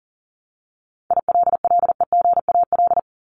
Morse code, heard as a keyed tone, about 40 wpm, 700 Hz. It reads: ILLEGAL
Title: Top 1751-2000 Words 40wpm